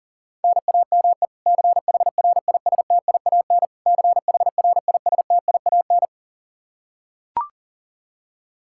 Morse code, heard as a keyed tone, about 40 wpm, 700 Hz. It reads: NAME CHRISTIAN CHRISTIAN
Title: Practice QSO - Exchange 1 - Beginning - 40wpm